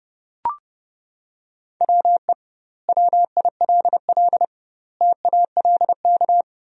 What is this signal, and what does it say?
Morse code, heard as a keyed tone, about 32 wpm, 700 Hz. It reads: WE WILL TALK
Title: 2x Word-Spacing - Sentences from Top 400 Words - 30wpm